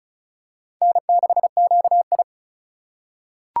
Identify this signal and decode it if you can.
Morse code, heard as a keyed tone, about 36 wpm, 700 Hz. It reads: N6QI
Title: USA Call Signs - 4 Characters - 8x Character Spacing - 35wpm